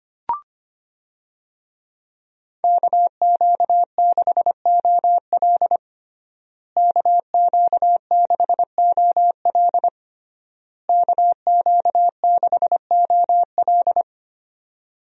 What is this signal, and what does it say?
Morse code, heard as a keyed tone, about 25 wpm, 700 Hz. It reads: KQ6OL KQ6OL KQ6OL
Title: USA Call Signs - 5 Characters - Repeated 3x - 25wpm